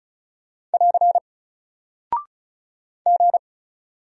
Morse code, heard as a keyed tone, about 36 wpm, 700 Hz. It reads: <AR> G
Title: Single Character (Letter:Number:Punctuation:Prosign) Alt 35wpm